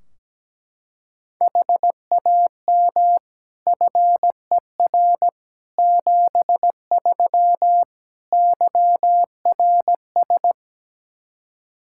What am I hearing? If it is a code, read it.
Morse code, heard as a keyed tone, about 17 wpm, 700 Hz. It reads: HAM FER 73 YRS
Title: Practice QSO - Exchange 3 - More Info - 17wpm